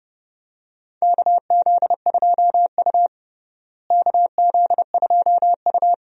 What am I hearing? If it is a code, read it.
Morse code, heard as a keyed tone, about 32 wpm, 700 Hz. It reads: KZ2U KZ2U
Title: USA Call Signs - 4 Characters - Repeated 2x - 30wpm